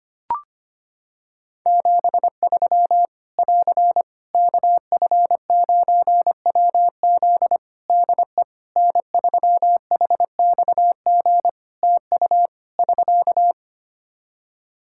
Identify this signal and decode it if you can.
Morse code, heard as a keyed tone, about 25 wpm, 700 Hz. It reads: 73 <AR> KF9WZ DE N3HXG TU <SK>
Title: Practice QSO - Exchange 4 - Ending - 25wpm